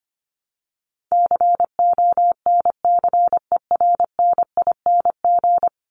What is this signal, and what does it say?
Morse code, heard as a keyed tone, about 25 wpm, 700 Hz. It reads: CONCERNING
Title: Top 1751-2000 Words 25wpm